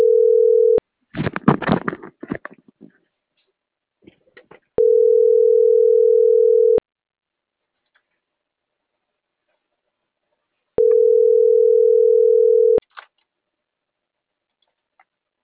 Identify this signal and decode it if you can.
Ringback tone (North America), 440 + 480 Hz.